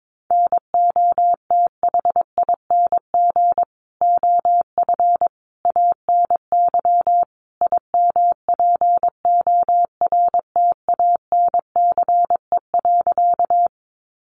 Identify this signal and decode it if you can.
Morse code, heard as a keyed tone, about 22 wpm, 700 Hz. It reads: NOTHING OF ANY IMPORTANCE.
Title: Sentences from Top 1200 Words 22wpm